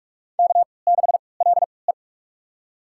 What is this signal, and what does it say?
Morse code, heard as a keyed tone, about 42 wpm, spaced as for 25 wpm, 690 Hz. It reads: K6LE
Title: USA Call Signs - 4 Characters - 3x Character Spacing - 45wpm